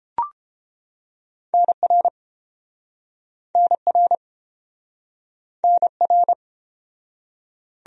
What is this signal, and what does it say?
Morse code, heard as a keyed tone, about 30 wpm, 700 Hz. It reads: NR NR NR